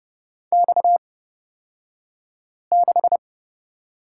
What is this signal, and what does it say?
Morse code, heard as a keyed tone, about 30 wpm, spaced as for 5 wpm, 700 Hz. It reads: X6
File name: Lesson 37d - Letter x - Two Characters 30wpm